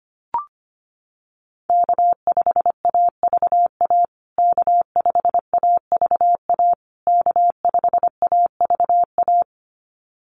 Morse code, heard as a keyed tone, about 25 wpm, 700 Hz. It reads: K5AVA K5AVA K5AVA
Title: USA Call Signs Repeated 3x 25wpm